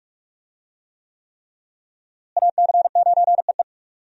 Morse code, heard as a keyed tone, about 46 wpm, 700 Hz. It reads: AC9EE
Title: USA Call Signs - 5 Characters - Repeated 3x - 45wpm